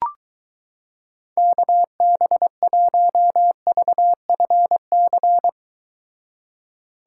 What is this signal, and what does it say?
Morse code, heard as a keyed tone, about 22 wpm, 700 Hz. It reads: KB1VFC